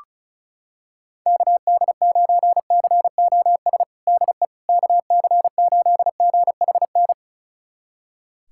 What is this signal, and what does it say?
Morse code, heard as a keyed tone, about 36 wpm, 700 Hz. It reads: KD9COS DE KC8GHN
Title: USA Callsign DE Callsign - 6 Characters - 35wpm